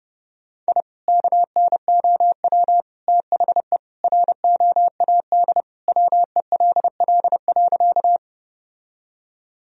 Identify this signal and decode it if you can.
Morse code, heard as a keyed tone, about 30 wpm, 700 Hz. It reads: I KNOW THE ROAD WELL.
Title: Sentences from Top 1000 Words - Review Part 2 - 30wpm